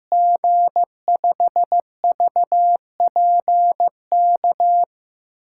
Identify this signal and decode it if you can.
Morse code, heard as a keyed tone, about 15 wpm, 700 Hz. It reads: G5VPK
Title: USA Call Signs - 6 Characters - 8x Character Spacing - 15wpm